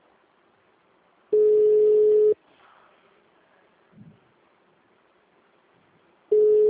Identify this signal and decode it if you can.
Ringback tone (Europe, 425 Hz).
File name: out- -2015- - - .0